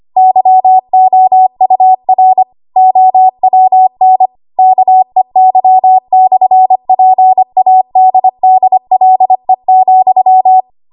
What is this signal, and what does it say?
Morse code, heard as a keyed tone, about 25 wpm, 750 Hz. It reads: YOUR OWN KEY/PADDLE,
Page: 25 WPM Code Archive